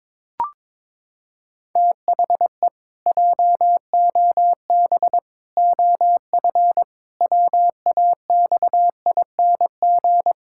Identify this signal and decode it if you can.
Morse code, heard as a keyed tone, about 22 wpm, 700 Hz. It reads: THE JOB OF WAXING
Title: Pangram Challenge v2 - 22wpm